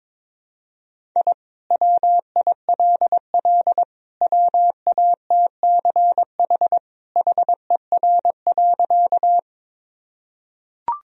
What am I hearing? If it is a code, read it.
Morse code, heard as a keyed tone, about 22 wpm, 700 Hz. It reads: I WILL WATCH HER.
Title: Sentences from Top 1000 Words - Review Part 1 - 22wpm